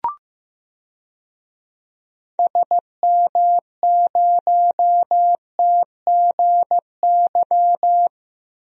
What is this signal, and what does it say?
Morse code, heard as a keyed tone, about 15 wpm, 700 Hz. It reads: SM0TGY
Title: DXCC-100 Call Signs - 6 Characters - Repeated 2x - 15wpm